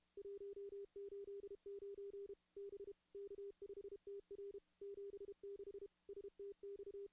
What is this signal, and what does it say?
Morse code, heard as a keyed tone, about 33 wpm, 400 Hz. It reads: 189 B K5TR 76 STX
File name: contests / CW-SS-k5tr- -2200